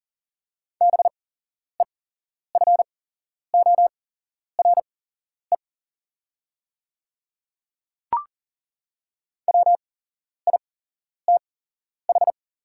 Morse code, heard as a keyed tone, about 42 wpm, spaced as for 11 wpm, 700 Hz. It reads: BEFORE WITH
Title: 8x Character Spacing - Top 200 Words - 40wpm